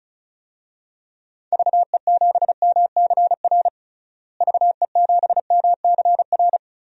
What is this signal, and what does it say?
Morse code, heard as a keyed tone, about 36 wpm, 700 Hz. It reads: VE7MCR VE7MCR
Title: DXCC-100 Call Signs - 6 Characters - Repeated 3x - 35wpm